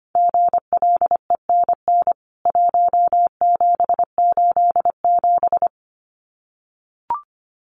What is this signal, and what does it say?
Morse code, heard as a keyed tone, about 25 wpm, 700 Hz. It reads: GLENN 1787
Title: CWT Exchange 25wpm